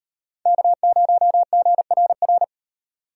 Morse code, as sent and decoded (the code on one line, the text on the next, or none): -.- ----- --. .-. .-.
K0GRR